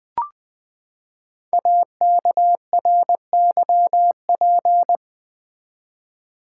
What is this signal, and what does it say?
Morse code, heard as a keyed tone, about 20 wpm, 700 Hz. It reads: AKRYP
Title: Code Groups - 5 Letters - 20wpm